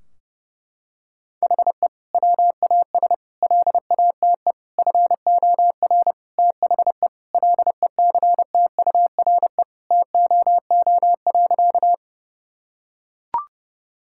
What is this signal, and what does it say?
Morse code, heard as a keyed tone, about 30 wpm, 700 Hz. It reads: HE WAS LATE FOR THE LECTURE TOO.
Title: Sentences from Top 1600 Words 30wpm